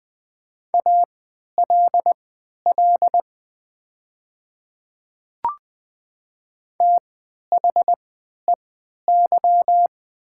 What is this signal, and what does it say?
Morse code, heard as a keyed tone, about 20 wpm, spaced as for 11 wpm, 700 Hz. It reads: ALL THEY
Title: 3x Character Spacing - Top 200 Words - 20wpm